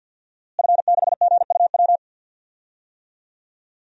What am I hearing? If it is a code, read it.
Morse code, heard as a keyed tone, about 46 wpm, 700 Hz. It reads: F6GUW
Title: DXCC-100 Call Signs - 5 Characters - Repeated 2x - 50wpm